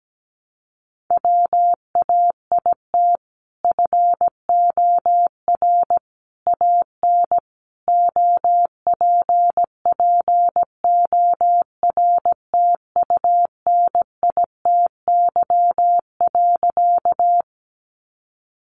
Morse code, heard as a keyed tone, about 17 wpm, 700 Hz. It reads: WAIT FOR AN OPPORTUNITY.
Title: Sentences from Top 1000 Words 17wpm